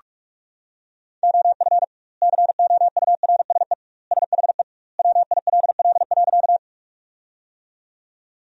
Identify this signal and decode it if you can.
Morse code, heard as a keyed tone, about 46 wpm, 700 Hz. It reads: OF COURSE SHE WILL.
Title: 2x Word-Spacing - Sentences from Top 500 Words - 45wpm